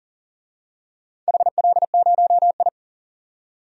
Morse code, heard as a keyed tone, about 40 wpm, 700 Hz. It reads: HL0I
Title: DXCC-100 Call Signs - 4 Characters - 40wpm